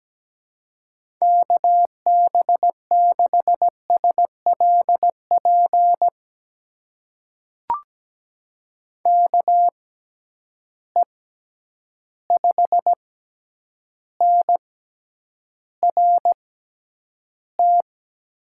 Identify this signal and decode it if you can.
Morse code, heard as a keyed tone, about 17 wpm, spaced as for 9 wpm, 700 Hz. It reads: KB6SLP KE5NRT
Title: USA Call Signs - 6 Characters - 6x Character Spacing - 17wpm